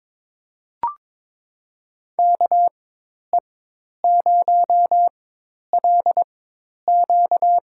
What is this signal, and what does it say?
Morse code, heard as a keyed tone, about 22 wpm, spaced as for 10 wpm, 700 Hz. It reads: KE0LQ
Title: USA Call Signs - 4x Character Spacing - 22wpm